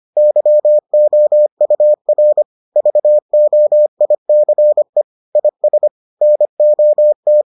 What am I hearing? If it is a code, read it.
Morse code, heard as a keyed tone, about 25 wpm, 600 Hz. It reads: YOUR VOICE IS NOT